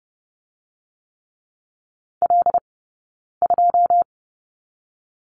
Morse code, heard as a keyed tone, about 30 wpm, 700 Hz. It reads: L 2
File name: 092 Lesson 16d - Letter c - Two Characters 30wpm